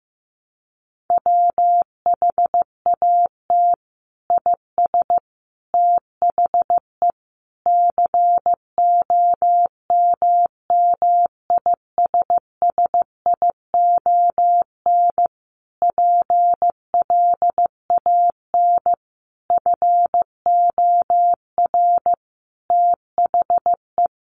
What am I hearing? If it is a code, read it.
Morse code, heard as a keyed tone, about 15 wpm, 700 Hz. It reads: WHAT IS THE COMMISSION PLAN FOR THE
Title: Sentences from Top 1400 Words 15wpm